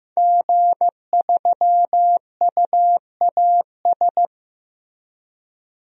Morse code, heard as a keyed tone, about 15 wpm, 700 Hz. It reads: G3UAS
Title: DXCC-100 Call Signs - 5 Characters - 15wpm